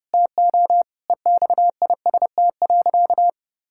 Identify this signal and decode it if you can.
Morse code, heard as a keyed tone, about 31 wpm, 700 Hz. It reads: TO EXIST.
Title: Sentences from Top 1000 Words - Review Part 5 - 30wpm